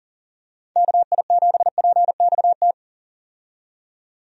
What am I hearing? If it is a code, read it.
Morse code, heard as a keyed tone, about 40 wpm, 700 Hz. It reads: KI7PXT
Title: USA Call Signs - 6 Characters - 40wpm